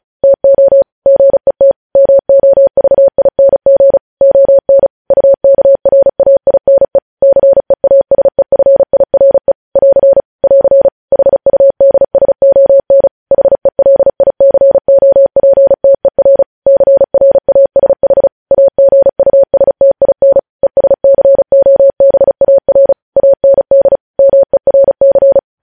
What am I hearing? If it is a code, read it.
Morse code, heard as a keyed tone, about 36 wpm, 550 Hz. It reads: TO GET MOVING ON UKRAINE CEASEFIRE <AR> <AR> HUDSON HELICOPTER CRASH AGUSTIN ESCOBAR AND MERC
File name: podcast / News headlines-1x-35x35